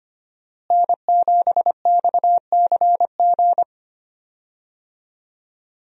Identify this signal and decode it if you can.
Morse code, heard as a keyed tone, about 25 wpm, 700 Hz. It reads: N7XCG
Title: USA Call Signs - 5 Characters - Repeated 3x - 25wpm